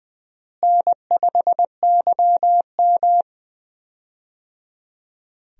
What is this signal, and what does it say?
Morse code, heard as a keyed tone, about 20 wpm, 690 Hz. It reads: N5YM
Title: USA Call Signs - 4 Characters - Repeated 3x - 20wpm